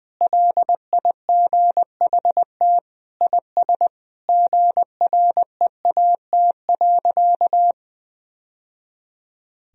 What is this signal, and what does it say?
Morse code, heard as a keyed tone, about 21 wpm, 700 Hz. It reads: LIGHT IS GREAT.
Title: Sentences from Top 1000 Words - Review Part 1 - 20wpm